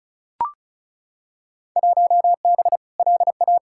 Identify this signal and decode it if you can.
Morse code, heard as a keyed tone, about 36 wpm, 700 Hz. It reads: 1B LA
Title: Field Day Exchanges Encore - 35wpm